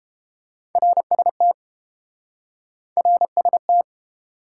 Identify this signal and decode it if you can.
Morse code, heard as a keyed tone, about 30 wpm, 700 Hz. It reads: RST RST